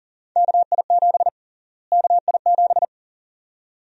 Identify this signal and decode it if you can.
Morse code, heard as a keyed tone, about 40 wpm, 700 Hz. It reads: KI7 KI7
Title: USA Call Sign Prefixes - Repeated 2x - 40wpm